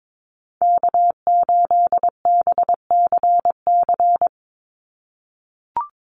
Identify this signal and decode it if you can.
Morse code, heard as a keyed tone, about 22 wpm, 700 Hz. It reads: K8BCC
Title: USA Call Signs 22wpm